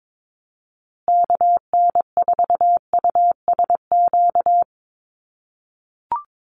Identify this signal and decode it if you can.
Morse code, heard as a keyed tone, about 22 wpm, 700 Hz. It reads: KN4USQ
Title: USA Call Signs Repeated 3x 22wpm